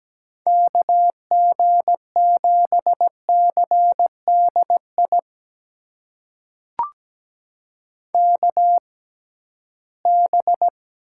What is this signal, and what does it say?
Morse code, heard as a keyed tone, about 17 wpm, spaced as for 13 wpm, 700 Hz. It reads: KG7CDI KB
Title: USA Call Signs - 6 Characters - 6x Character Spacing - 17wpm